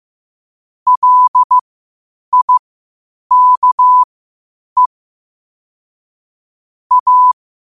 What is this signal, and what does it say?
Morse code, heard as a keyed tone, about 15 wpm, spaced as for 8 wpm, 1000 Hz. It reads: LIKE A
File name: qotdfiles / 8wpm